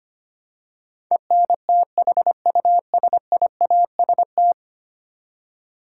Morse code, heard as a keyed tone, about 25 wpm, 700 Hz. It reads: ENTHUSIAST
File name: Ten Letter Words - 25wpm